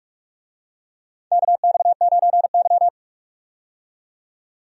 Morse code, heard as a keyed tone, about 42 wpm, 700 Hz. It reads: KX9Y
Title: USA Call Signs - 4 Characters - 45wpm